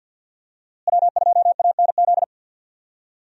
Morse code, heard as a keyed tone, about 50 wpm, 700 Hz. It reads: W2ANZ
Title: USA Call Signs - 5 Characters - Repeated 3x - 50wpm